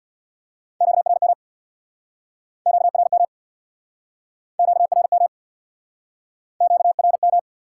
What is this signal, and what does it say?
Morse code, heard as a keyed tone, about 69 wpm, 700 Hz. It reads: QRM QRM QRM QRM